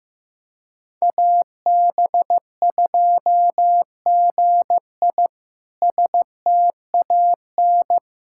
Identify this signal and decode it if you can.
Morse code, heard as a keyed tone, about 16 wpm, 700 Hz. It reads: AB2GI STAN
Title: K1USN SST QSO Practice - 15wpm